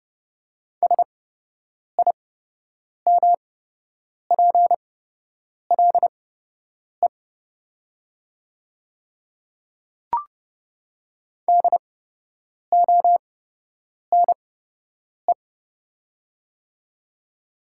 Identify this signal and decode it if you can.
Morse code, heard as a keyed tone, about 32 wpm, spaced as for 8 wpm, 700 Hz. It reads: SIMPLE DONE
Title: 8x Character Spacing - Top 500 Words - 30wpm